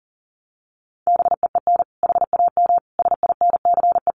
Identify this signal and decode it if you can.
Morse code, heard as a keyed tone, about 40 wpm, 700 Hz. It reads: BEEN HAM SINCE
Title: Common QSO Phrases 40wpm